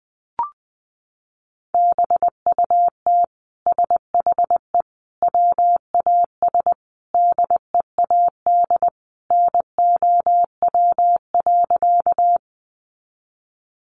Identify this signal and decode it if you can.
Morse code, heard as a keyed tone, about 20 wpm, 700 Hz. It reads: BUT SHE WAS DEAD NOW.
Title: Sentences from Top 1500 Words 20wpm